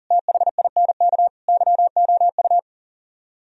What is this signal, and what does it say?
Morse code, heard as a keyed tone, about 52 wpm, 700 Hz. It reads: THINK YOU